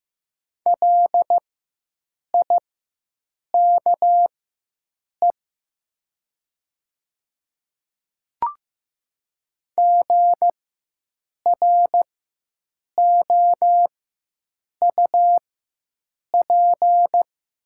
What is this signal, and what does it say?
Morse code, heard as a keyed tone, about 15 wpm, spaced as for 7 wpm, 700 Hz. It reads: LIKE GROUP
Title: 4x Character Spacing - Top 300 Words - 15wpm